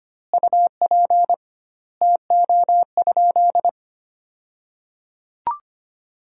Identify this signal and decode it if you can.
Morse code, heard as a keyed tone, about 25 wpm, 700 Hz. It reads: UP TO?
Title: 2x Word-Spacing - Sentences from Top 100 Words - 25wpm